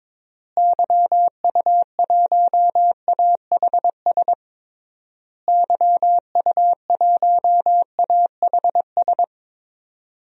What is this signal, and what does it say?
Morse code, heard as a keyed tone, about 22 wpm, 700 Hz. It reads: YU1AHS YU1AHS
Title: DXCC-100 Call Signs - 6 Characters - Repeated 3x - 22wpm